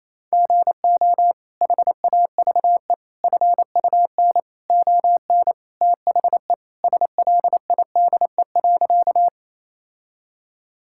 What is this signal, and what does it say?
Morse code, heard as a keyed tone, about 28 wpm, 700 Hz. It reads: GO HAVE FUN ON THE SLIDE.
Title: Sentences from Top 1600 Words 28wpm